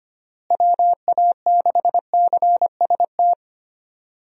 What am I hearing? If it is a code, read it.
Morse code, heard as a keyed tone, about 25 wpm, 700 Hz. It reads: WA6CST
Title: USA Call Signs - 6 Characters - Repeated 3x - 25wpm